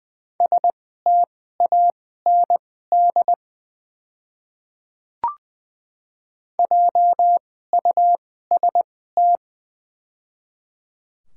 Morse code, heard as a keyed tone, about 20 wpm, spaced as for 14 wpm, 700 Hz. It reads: STAND JUST
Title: 2x Character Spacing - Top 300 Words - 20wpm